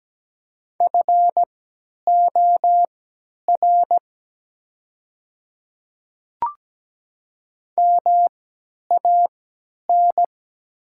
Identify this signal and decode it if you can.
Morse code, heard as a keyed tone, about 17 wpm, spaced as for 10 wpm, 690 Hz. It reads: FOR MAN
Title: 3x Character Spacing - Top 300 Words - 17wpm